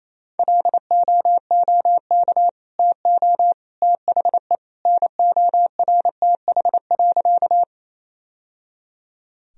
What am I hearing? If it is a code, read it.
Morse code, heard as a keyed tone, about 28 wpm, 700 Hz. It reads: LOOK TO THE NORTH.
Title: Sentences from Top 1000 Words - Review Part 1 - 28wpm